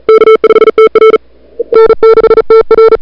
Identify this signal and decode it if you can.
Morse code, heard as a keyed tone, about 37 wpm, 440 Hz. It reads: K5TR N6TR
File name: CWSprint / February 2007 CW Sprint / 01-2007 CW Sprint Hour1